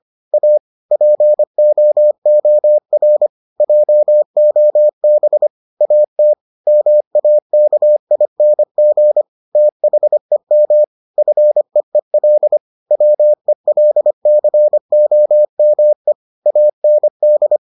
Morse code, heard as a keyed tone, about 25 wpm, 600 Hz. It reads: A POOR JOB AT MAKING THEM FEEL WELCOME AND